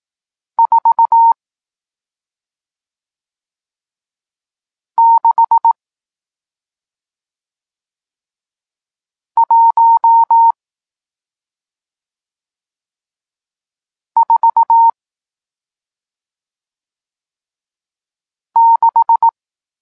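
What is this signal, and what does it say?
Morse code, heard as a keyed tone, about 18 wpm, spaced as for 2 wpm, 920 Hz. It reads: 46146
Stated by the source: Morse code lesson teaching digits 1, 4 and 6, from SP9MOA, Niepołomicki Klub Krótkofalowców (Poland) amateur radio club.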